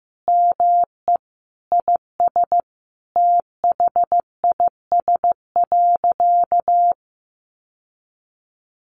Morse code, heard as a keyed tone, about 15 wpm, 700 Hz. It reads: ME IS THIS.
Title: Sentences from Top 1200 Words 15wpm